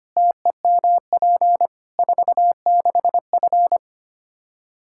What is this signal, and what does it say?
Morse code, heard as a keyed tone, about 25 wpm, 700 Hz. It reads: TEMP 46F
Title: Practice QSO - Exchange 2 - Info - 25wpm